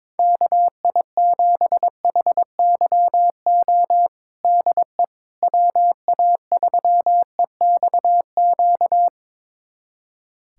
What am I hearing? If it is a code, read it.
Morse code, heard as a keyed tone, about 22 wpm, 700 Hz. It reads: KI7HYO DE WA3EXQ
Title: USA Callsign DE Callsign - 6 Characters - 22wpm